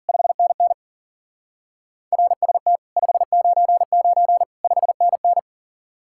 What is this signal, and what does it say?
Morse code, heard as a keyed tone, about 48 wpm, 700 Hz. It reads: 5NN RST 599 5NN
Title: Speed-Racing Common QSO Phrases Encore - 40wpm